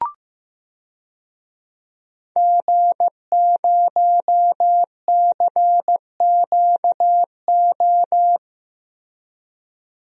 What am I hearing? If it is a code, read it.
Morse code, heard as a keyed tone, about 15 wpm, 700 Hz. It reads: G0CQO
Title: DXCC-100 Call Signs - 5 Characters - Repeated 2x - 15wpm